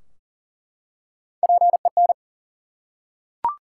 Morse code, heard as a keyed tone, about 39 wpm, 700 Hz. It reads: PEN
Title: Three Letter Words - 40wpm